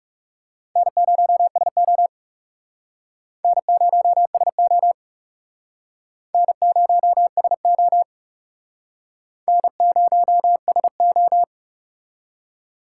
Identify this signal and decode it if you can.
Morse code, heard as a keyed tone, about 37 wpm, 700 Hz. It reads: N0SO N0SO N0SO N0SO
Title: Speed Racing - USA 1x2 Call Signs 30wpm